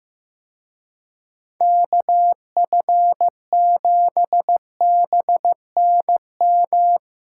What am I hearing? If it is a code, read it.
Morse code, heard as a keyed tone, about 16 wpm, 700 Hz. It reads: KF7BNM
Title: Speed Racing - USA 2x3 Call Signs 15wpm